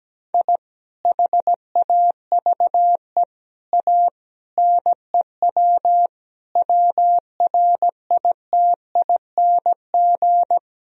Morse code, heard as a keyed tone, about 17 wpm, 700 Hz. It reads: I HAVE A NEW WRITING
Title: Sentences from Top 1000 Words - Review Part 3 - 17wpm